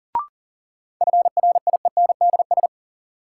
Morse code, heard as a keyed tone, about 40 wpm, 700 Hz. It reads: FRIENDS
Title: Sentences from Top 1000 Words - Review Part 5 - New Word Drill - 40wpm